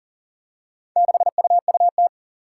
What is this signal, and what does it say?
Morse code, heard as a keyed tone, about 40 wpm, 700 Hz. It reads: 6UUT
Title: Four Challenging Letter-Number Combinations 40wpm